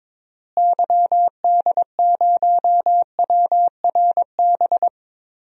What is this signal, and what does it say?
Morse code, heard as a keyed tone, about 22 wpm, 700 Hz. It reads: YD0WRB
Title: DXCC-100 Call Signs - 6 Characters - Repeated 3x - 22wpm